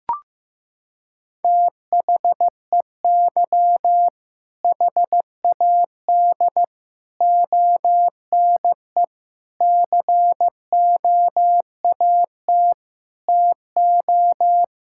Morse code, heard as a keyed tone, about 15 wpm, 700 Hz. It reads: THEY HAD ONE COAT TO
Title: Sentences from Top 1600 Words 15wpm